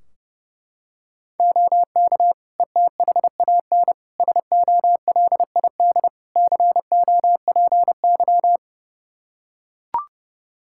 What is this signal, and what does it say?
Morse code, heard as a keyed tone, about 32 wpm, 700 Hz. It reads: OK ETHAN SOLID COPY
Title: Practice QSO - Exchange 3 - More Info - 30wpm